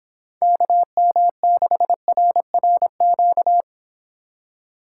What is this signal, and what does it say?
Morse code, heard as a keyed tone, about 26 wpm, 700 Hz. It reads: KM6RRQ